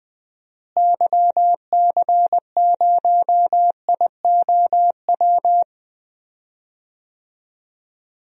Morse code, heard as a keyed tone, about 21 wpm, 700 Hz. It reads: YC0IOW
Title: DXCC-100 Call Signs - 6 Characters - Repeated 3x - 20wpm